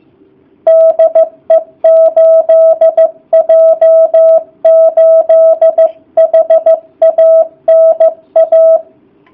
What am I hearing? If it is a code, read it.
Morse code, heard as a keyed tone, about 14 wpm, 650 Hz. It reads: DE 8J8HANA